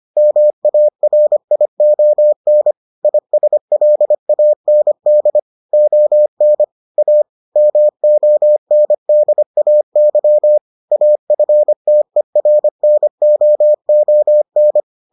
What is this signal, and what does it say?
Morse code, heard as a keyed tone, about 25 wpm, 600 Hz. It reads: MARION ISLAND ON A MONDAY AFTERNOON